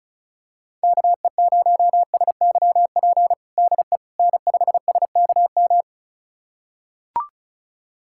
Morse code, heard as a keyed tone, about 36 wpm, 700 Hz. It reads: KE0SYP DE N5SKM
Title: USA Callsign DE Callsign 35wpm